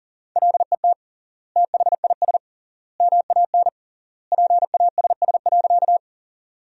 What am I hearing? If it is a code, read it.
Morse code, heard as a keyed tone, about 37 wpm, 700 Hz. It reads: LET THIS MAN PASS.
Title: 3x Word-Spacing - Sentences from Top 400 Words - 40wpm